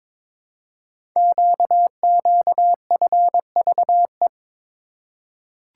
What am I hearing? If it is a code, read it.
Morse code, heard as a keyed tone, about 22 wpm, 700 Hz. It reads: QQFVE
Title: Code Groups - 5 Letter:Numbers - 22wpm